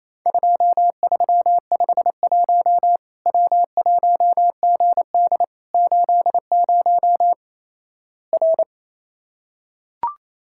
Morse code, heard as a keyed tone, about 27 wpm, 700 Hz. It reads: 2351 W1GD 80 R
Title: Worked All Europe DX Contest - QTC Exchange - 28wpm